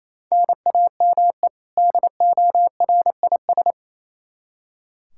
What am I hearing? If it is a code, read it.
Morse code, heard as a keyed tone, about 28 wpm, 700 Hz. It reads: NAME DORIS
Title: Practice QSO - Exchange 1 - Beginning - 28wpm